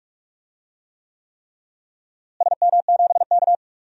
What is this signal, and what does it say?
Morse code, heard as a keyed tone, about 46 wpm, 700 Hz. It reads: SM7K
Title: DXCC-100 Call Signs - 4 Characters - Repeated 3x - 45wpm